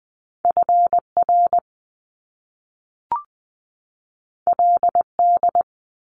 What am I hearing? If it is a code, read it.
Morse code, heard as a keyed tone, about 20 wpm, 700 Hz. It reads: FR LD
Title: Top Two Letter Combinations in Words 20wpm